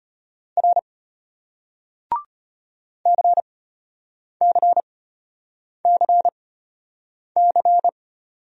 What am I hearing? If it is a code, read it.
Morse code, heard as a keyed tone, about 31 wpm, spaced as for 8 wpm, 700 Hz. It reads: R CCCC
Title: Speed-Racing Single Letters 25wpm